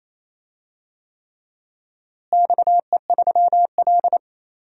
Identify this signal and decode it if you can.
Morse code, heard as a keyed tone, about 28 wpm, 700 Hz. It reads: XE3L
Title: DXCC-100 Call Signs - 4 Characters - Repeated 3x - 28wpm